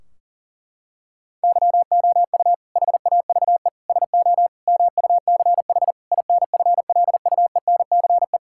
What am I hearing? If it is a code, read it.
Morse code, heard as a keyed tone, about 40 wpm, 700 Hz. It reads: YOU HAVE SO MUCH INFLUENCE
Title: Sentences from Top 1600 Words 40wpm